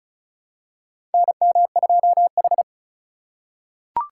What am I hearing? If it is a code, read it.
Morse code, heard as a keyed tone, about 36 wpm, 700 Hz. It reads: NM2H
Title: USA Call Signs - 4 Characters - 3x Character Spacing - 35wpm